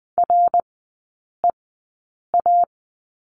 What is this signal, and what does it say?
Morse code, heard as a keyed tone, about 20 wpm, spaced as for 8 wpm, 700 Hz. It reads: REA
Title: 100 Lesson 17e - Letter u - Three Characters 20wpm